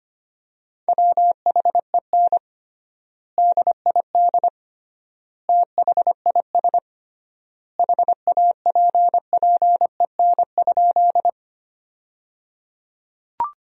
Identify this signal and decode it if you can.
Morse code, heard as a keyed tone, about 25 wpm, 700 Hz. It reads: WHEN DID THIS HAPPEN?
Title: 3x Word-Spacing - Sentences from Top 400 Words - 25wpm